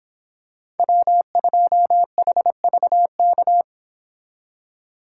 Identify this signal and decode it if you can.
Morse code, heard as a keyed tone, about 26 wpm, 700 Hz. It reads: W2HVK